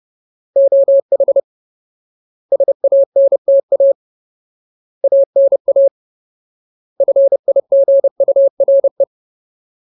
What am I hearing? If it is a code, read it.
Morse code, heard as a keyed tone, about 30 wpm, 550 Hz. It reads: OH SANTA ANA FIGURE
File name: Ultimate Word List / 30WPM Ultimate